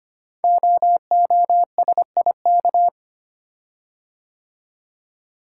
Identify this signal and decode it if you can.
Morse code, heard as a keyed tone, about 25 wpm, 700 Hz. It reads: OOSIK
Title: Code Groups - 5 Letters - 25wpm